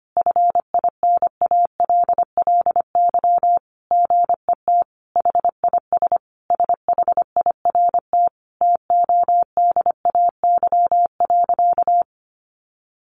Morse code, heard as a keyed tone, about 25 wpm, 700 Hz. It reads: FINALLY GET HIS SHIRT TODAY.
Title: Additional Easy Phrases 25wpm